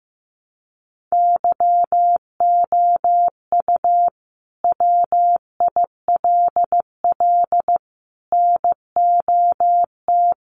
Morse code, heard as a keyed tone, about 15 wpm, 700 Hz. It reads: YOU WILL NOT
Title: Sentences from Top 1300 Words 15wpm